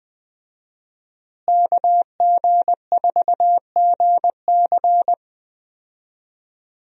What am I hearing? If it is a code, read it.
Morse code, heard as a keyed tone, about 20 wpm, 700 Hz. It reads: KG4GC
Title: USA Call Signs - 5 Characters - Repeated 3x - 20wpm